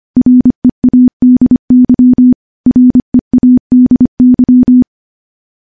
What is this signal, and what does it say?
Morse code, heard as a keyed tone, about 25 wpm, 250 Hz. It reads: READY READY
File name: morse-training-25WPM